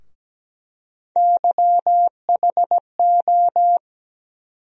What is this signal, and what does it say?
Morse code, heard as a keyed tone, about 17 wpm, 700 Hz. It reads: YHO
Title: USA Call Sign Postfixes - 3 Letters 17wpm